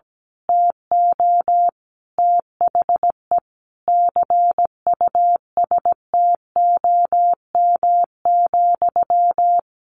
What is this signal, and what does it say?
Morse code, heard as a keyed tone, about 17 wpm, 700 Hz. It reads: TO THE CUSTOM,
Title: A Sound Of Thunder 17wpm